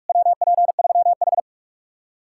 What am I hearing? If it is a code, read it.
Morse code, heard as a keyed tone, about 59 wpm, spaced as for 52 wpm, 700 Hz. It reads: WP3H
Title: Speed-Racing Common QSO Phrases Encore - 45wpm